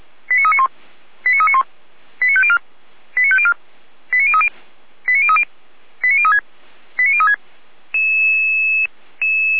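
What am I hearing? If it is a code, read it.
Five-tone selective call (ZVEI), heard as quick runs of five tones, tones of about 80 ms. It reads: 89391 89391 89504 89504 89030 89030 89036 89036
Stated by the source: ZVEI five-tone call received by SDR.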